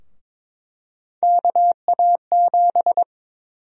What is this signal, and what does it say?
Morse code, heard as a keyed tone, about 21 wpm, 700 Hz. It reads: KA7
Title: USA Call Sign Prefixes - 22wpm